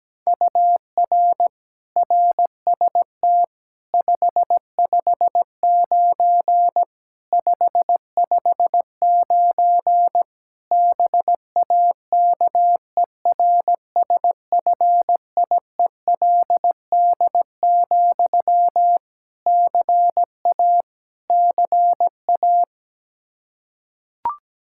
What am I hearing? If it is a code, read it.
Morse code, heard as a keyed tone, about 17 wpm, 700 Hz. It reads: UR RST 559 559 BAKERSFIELD, CA CA
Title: K1USN SST QSO Practice 17wpm